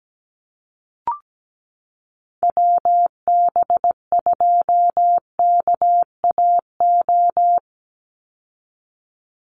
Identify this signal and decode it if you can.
Morse code, heard as a keyed tone, about 17 wpm, 700 Hz. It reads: WB2KAO
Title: Call Signs Repeated 3x - 17wpm